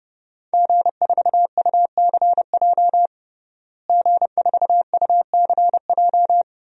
Morse code, heard as a keyed tone, about 30 wpm, 700 Hz. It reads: G4UCJ G4UCJ
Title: DXCC-100 Call Signs - 5 Characters - Repeated 2x - 30wpm